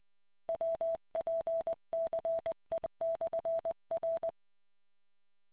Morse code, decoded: WPCE/R